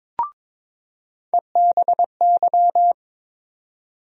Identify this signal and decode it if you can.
Morse code, heard as a keyed tone, about 22 wpm, 700 Hz. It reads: EBY
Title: USA Call Sign Postfixes - 3 Letters 22wpm